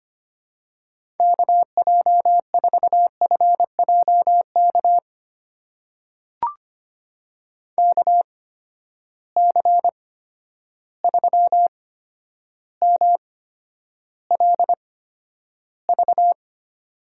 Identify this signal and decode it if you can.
Morse code, heard as a keyed tone, about 25 wpm, spaced as for 11 wpm, 700 Hz. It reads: KJ4FJK KC3MLV
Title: USA Call Signs - 6 Characters - 8x Character Spacing - 25wpm